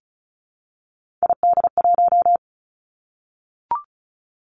Morse code, heard as a keyed tone, about 36 wpm, 700 Hz. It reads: ID1
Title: Three Challenging Letter-Number Combinations 35wpm